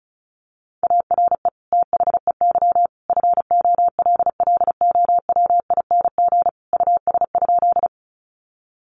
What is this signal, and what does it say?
Morse code, heard as a keyed tone, about 34 wpm, 700 Hz. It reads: ARE THEY FOLLOWING US?
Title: Sentences from Top 700 Words 35wpm